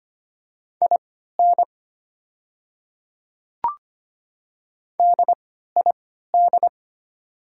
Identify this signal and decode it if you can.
Morse code, heard as a keyed tone, about 25 wpm, spaced as for 14 wpm, 700 Hz. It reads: IN DID
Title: 3x Character Spacing - Top 100 Words - 25wpm